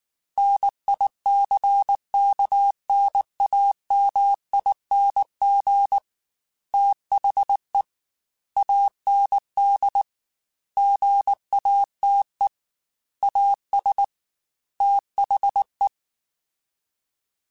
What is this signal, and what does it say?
Morse code, heard as a keyed tone, about 19 wpm, 800 Hz. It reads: NICKNAMING THE AND GATE AS THE
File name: Technical Subjects / Technical Subjects-mp3s / AND Gate @18